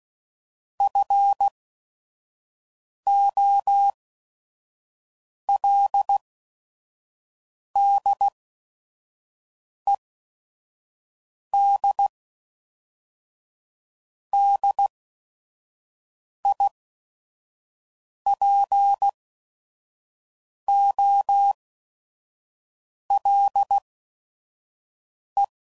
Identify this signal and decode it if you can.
Morse code, heard as a keyed tone, about 16 wpm, spaced as for 5 wpm, 800 Hz. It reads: FOLDED DIPOLE
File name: CW Code Groups @5WPM / fromSM's / Antennas